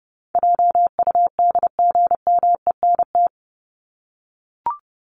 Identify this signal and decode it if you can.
Morse code, heard as a keyed tone, about 30 wpm, 700 Hz. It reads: JUDGMENT
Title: Top 1751-2000 Words 30wpm